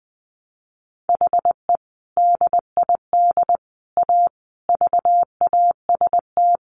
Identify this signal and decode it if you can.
Morse code, heard as a keyed tone, about 20 wpm, 700 Hz. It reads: HE DID A VAST
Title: Sentences from Top 1300 Words 20wpm